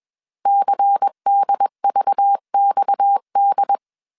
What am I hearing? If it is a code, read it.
Morse code, heard as a keyed tone, about 21 wpm, 800 Hz. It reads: CDVXD
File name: CW / DET / 1 6 1500C S B 20wpm 2048